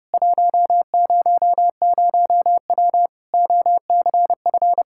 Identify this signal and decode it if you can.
Morse code, heard as a keyed tone, about 32 wpm, 700 Hz. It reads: 100W OCF
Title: Practice QSO - Exchange 2 - Info - 30wpm